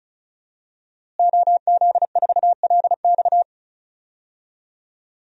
Morse code, heard as a keyed tone, about 34 wpm, 700 Hz. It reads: OZ4LX
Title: DXCC-100 Call Signs - 5 Characters - Repeated 2x - 35wpm